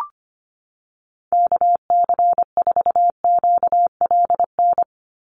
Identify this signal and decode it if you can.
Morse code, heard as a keyed tone, about 25 wpm, 700 Hz. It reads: KC4QLN
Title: USA Call Signs 25wpm